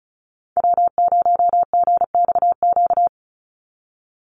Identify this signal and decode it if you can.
Morse code, heard as a keyed tone, about 34 wpm, 700 Hz. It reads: W0GXQ
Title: Call Signs - 35wpm